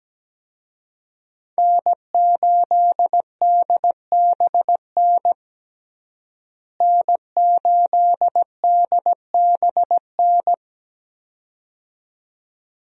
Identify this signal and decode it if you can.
Morse code, heard as a keyed tone, about 17 wpm, 700 Hz. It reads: N8DBN N8DBN
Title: USA Call Signs - 5 Characters - Repeated 2x - 17wpm